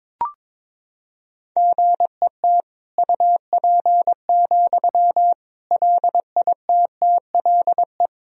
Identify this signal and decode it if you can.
Morse code, heard as a keyed tone, about 22 wpm, 700 Hz. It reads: GET UP, LITTLE
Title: Sentences from Top 1000 Words - Review Part 1 - 22wpm